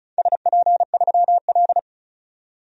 Morse code, heard as a keyed tone, about 36 wpm, 690 Hz. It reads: SP3L
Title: DXCC-100 Call Signs - 4 Characters - 35wpm